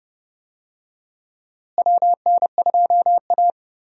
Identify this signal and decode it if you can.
Morse code, heard as a keyed tone, about 29 wpm, 700 Hz. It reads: WN2A
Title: USA Call Signs - 4 Characters - Repeated 2x - 30wpm